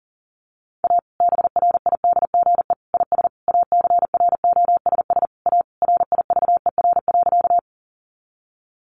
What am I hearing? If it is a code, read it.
Morse code, heard as a keyed tone, about 40 wpm, 700 Hz. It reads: A BRIDGE IS ACROSS A RIVER.
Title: Sentences from Top 1200 Words 40wpm